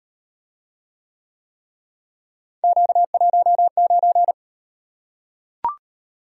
Morse code, heard as a keyed tone, about 38 wpm, 700 Hz. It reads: Q19